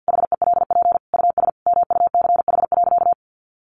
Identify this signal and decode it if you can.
Morse code, heard as a keyed tone, about 50 wpm, 700 Hz. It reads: HELP US MUCH.
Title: Sentences from Top 1500 Words 50wpm